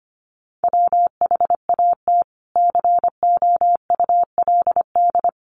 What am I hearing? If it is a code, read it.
Morse code, heard as a keyed tone, about 25 wpm, 700 Hz. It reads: WHAT COULD
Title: Sentences from Top 100 Words 25wpm